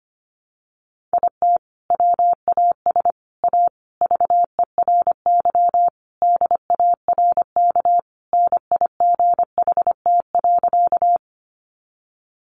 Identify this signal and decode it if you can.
Morse code, heard as a keyed tone, about 25 wpm, 700 Hz. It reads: IT WAS A VERY DARK NIGHT.
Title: Sentences from Top 500 Words 25wpm